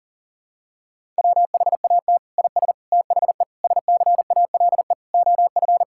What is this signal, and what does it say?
Morse code, heard as a keyed tone, about 40 wpm, 700 Hz. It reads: WHAT IS THE SCALE OF
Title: Sentences from Top 1000 Words - Review Part 5 - 40wpm